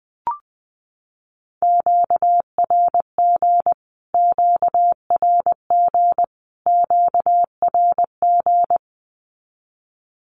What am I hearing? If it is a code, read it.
Morse code, heard as a keyed tone, about 20 wpm, 700 Hz. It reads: QRG QRG QRG
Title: Q-Codes Repeated 3x 20wpm